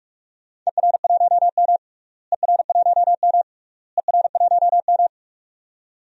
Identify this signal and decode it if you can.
Morse code, heard as a keyed tone, about 46 wpm, 700 Hz. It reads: ER1M ER1M ER1M
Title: DXCC-100 Call Signs - 4 Characters - Repeated 3x - 45wpm